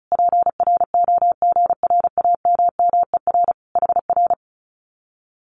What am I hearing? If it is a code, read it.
Morse code, heard as a keyed tone, about 34 wpm, 700 Hz. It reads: PROGRAMMER HR
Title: Common QSO Phrases Encore 35wpm